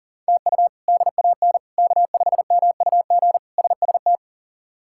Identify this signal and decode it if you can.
Morse code, heard as a keyed tone, about 37 wpm, 700 Hz. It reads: TU DAN K5MUG SST
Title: K1USN SST QSO Practice - 40wpm